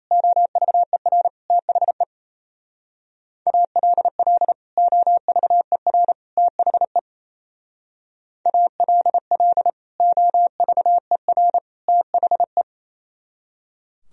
Morse code, heard as a keyed tone, about 33 wpm, 700 Hz. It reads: OVER THE ALL OVER THE ALL OVER THE
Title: Speed-Racing - Sets of 3 Words 28wpm